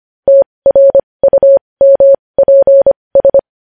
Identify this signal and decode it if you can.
Morse code, heard as a keyed tone, about 25 wpm, spaced as for 20 wpm, 550 Hz. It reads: TRUMPS
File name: podcast / News headlines-1x-25x20